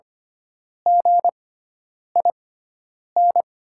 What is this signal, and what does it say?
Morse code, heard as a keyed tone, about 25 wpm, spaced as for 9 wpm, 700 Hz. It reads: GIN
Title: 6x Character Spacing - Top 300 Words - 25wpm